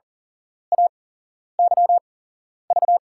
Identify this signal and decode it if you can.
Morse code, heard as a keyed tone, about 39 wpm, spaced as for 11 wpm, 700 Hz. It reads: AYV
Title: USA Call Signs - 5 Characters - 8x Character Spacing - 40wpm